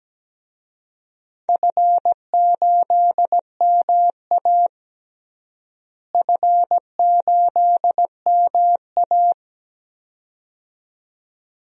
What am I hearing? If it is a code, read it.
Morse code, heard as a keyed tone, about 17 wpm, 700 Hz. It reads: F8MA F8MA
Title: DXCC-100 Call Signs - 4 Characters - Repeated 2x - 17wpm